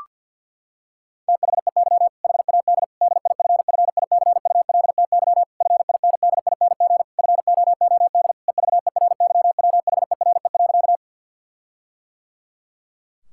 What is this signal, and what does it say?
Morse code, heard as a keyed tone, about 51 wpm, 700 Hz. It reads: THEY HAD DIFFICULTY FINDING FOOD EVERYWHERE.
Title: Sentences from Top 1600 Words 50wpm